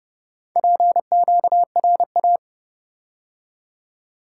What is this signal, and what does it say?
Morse code, heard as a keyed tone, about 30 wpm, 700 Hz. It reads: PQRA